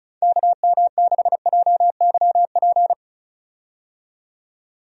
Morse code, heard as a keyed tone, about 36 wpm, 700 Hz. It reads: KM6JYP
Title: USA Call Signs - 6 Characters - 35wpm